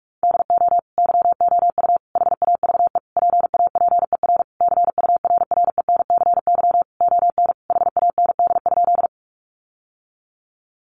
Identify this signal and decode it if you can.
Morse code, heard as a keyed tone, about 44 wpm, 700 Hz. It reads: DO YOU HAVE PAPER CURRENCY ON HAND?
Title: Sentences from Top 1500 Words 45wpm